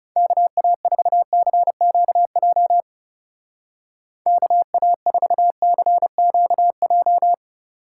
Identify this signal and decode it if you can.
Morse code, heard as a keyed tone, about 36 wpm, 700 Hz. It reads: KA4CQJ KA4CQJ
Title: Speed Racing - USA 2x3 Call Signs 30wpm